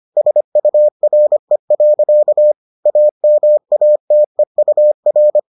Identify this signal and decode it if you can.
Morse code, heard as a keyed tone, about 25 wpm, 600 Hz. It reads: SURE. AMATEUR